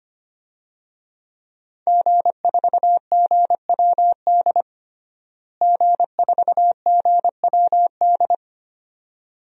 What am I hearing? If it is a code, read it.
Morse code, heard as a keyed tone, about 25 wpm, 700 Hz. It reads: G4GWD G4GWD
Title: DXCC-100 Call Signs - 5 Characters - Repeated 2x - 25wpm